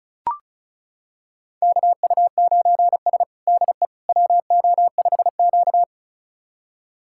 Morse code, heard as a keyed tone, about 36 wpm, 700 Hz. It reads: KU9S DE WO5Q
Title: USA Callsign DE Callsign - 4 Characters - 35wpm